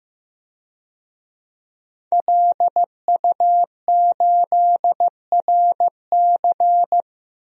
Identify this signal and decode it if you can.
Morse code, heard as a keyed tone, about 15 wpm, 700 Hz. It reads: LU8RC
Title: DXCC-100 Call Signs - 5 Characters - 15wpm